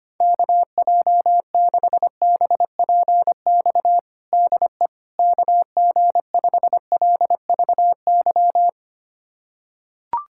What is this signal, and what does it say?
Morse code, heard as a keyed tone, about 25 wpm, 700 Hz. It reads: KJ6BPX DE KG5LVY
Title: USA Callsign DE Callsign - 6 Characters - 25wpm